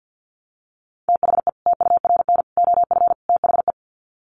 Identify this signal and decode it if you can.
Morse code, heard as a keyed tone, about 50 wpm, 700 Hz. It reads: THE TURN OF THE
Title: Sets of 4 Words 50wpm